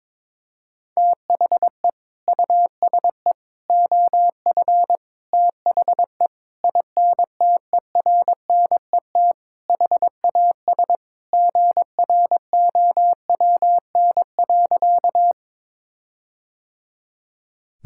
Morse code, heard as a keyed tone, about 22 wpm, 700 Hz. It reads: THE USE OF THE INTERNET HAS GROWN.
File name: practice / 22wpm / Sentences with 4-Word Phrases - 22wpm